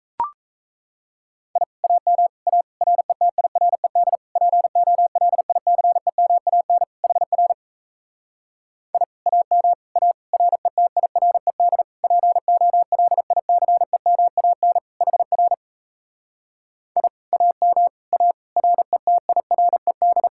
Morse code, heard as a keyed tone, about 38 wpm, 700 Hz. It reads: I AM A RETIRED POLICEMAN HR I AM A RETIRED POLICEMAN HR I AM A RETIRED